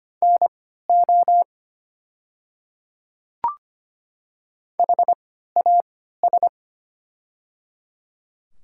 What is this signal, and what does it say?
Morse code, heard as a keyed tone, about 25 wpm, spaced as for 14 wpm, 700 Hz. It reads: NO HAS
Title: 3x Character Spacing - Top 100 Words - 25wpm